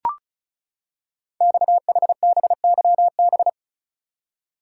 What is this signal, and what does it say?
Morse code, heard as a keyed tone, about 35 wpm, 700 Hz. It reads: XHBYB
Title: Code Groups - 5 Letters - 35wpm